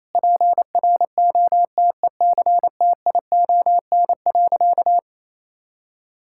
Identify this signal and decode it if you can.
Morse code, heard as a keyed tone, about 28 wpm, 700 Hz. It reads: PROTECTION.
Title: Sentences from Top 1600 Words 28wpm